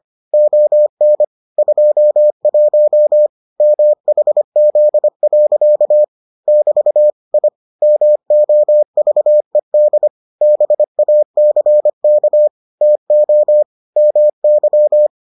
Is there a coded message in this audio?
Yes: Morse code from 0.3 to 15.1 s, heard as a keyed tone, about 25 wpm, 600 Hz. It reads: ON 21 MHZ. <BT> I MOVED BACK TO MY